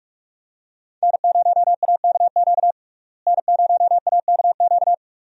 Morse code, heard as a keyed tone, about 41 wpm, 700 Hz. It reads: N0AKQ N0AKQ
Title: USA Call Signs - 5 Characters - Repeated 3x - 45wpm